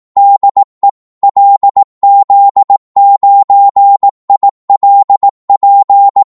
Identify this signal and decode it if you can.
Morse code, heard as a keyed tone, about 18 wpm, 800 Hz. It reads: DE LZ9ILP
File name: Morse / qso-18wpm